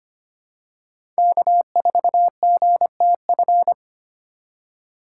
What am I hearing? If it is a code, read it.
Morse code, heard as a keyed tone, about 25 wpm, 700 Hz. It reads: K4GTF
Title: USA Call Signs - 5 Characters - 4x Character Spacing - 25wpm